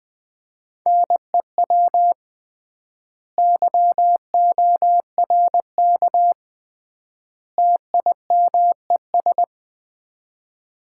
Morse code, heard as a keyed tone, about 21 wpm, 700 Hz. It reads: NEW YORK TIMES
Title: 3x Word-Spacing - Sets of 3 Words - 20wpm